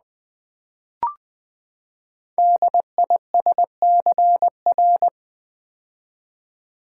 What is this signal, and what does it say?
Morse code, heard as a keyed tone, about 20 wpm, 700 Hz. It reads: DISCR
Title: Build A Word - 5 Letters At A Time - 10-Letter Words - 20wpm